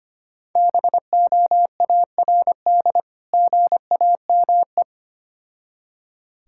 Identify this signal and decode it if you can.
Morse code, heard as a keyed tone, about 25 wpm, 700 Hz. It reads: BOARD GAME
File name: practice / 25wpm / Difficult Words with Hints - 25wpm